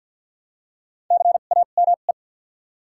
Morse code, heard as a keyed tone, about 52 wpm, 700 Hz. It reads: C A M E
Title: 2x Character Spacing - Top 200 Words - 50wpm